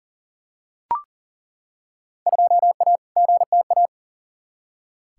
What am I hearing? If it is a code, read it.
Morse code, heard as a keyed tone, about 40 wpm, 700 Hz. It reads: 2A GTA
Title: Field Day Exchanges Encore - 40wpm